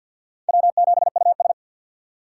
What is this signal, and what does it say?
Morse code, heard as a keyed tone, about 52 wpm, 700 Hz. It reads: W7US
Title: USA Call Signs - 3x Character Spacing - 50wpm